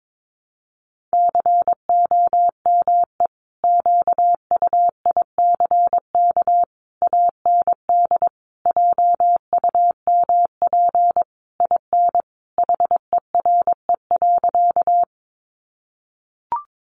Morse code, heard as a keyed tone, about 22 wpm, 700 Hz. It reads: COME QUICK AND JUMP IN HERE.
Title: Sentences from Top 1300 Words 22wpm